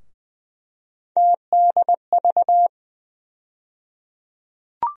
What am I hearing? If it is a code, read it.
Morse code, heard as a keyed tone, about 20 wpm, 700 Hz. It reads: TDV
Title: USA Call Sign Postfixes - 3 Letters 20wpm